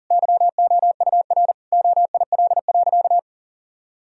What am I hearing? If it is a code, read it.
Morse code, heard as a keyed tone, about 37 wpm, 700 Hz. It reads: YOUR OIL.
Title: Sentences from Top 1000 Words - Review Part 4 - 40wpm